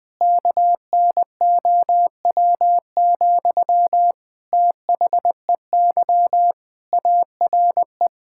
Morse code, heard as a keyed tone, about 20 wpm, 700 Hz. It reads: KNOW, THEY ARE
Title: Sentences from Top 1000 Words - Review Part 5 - 20wpm